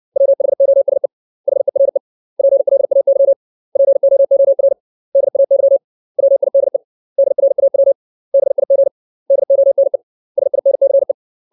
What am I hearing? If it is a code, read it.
Morse code, heard as a keyed tone, about 60 wpm, 550 Hz. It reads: WHOSE HERE PLAY POOR DAY WIDE DRAW BIG DONE SINCE
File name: 632 words / 632 Words 60WPM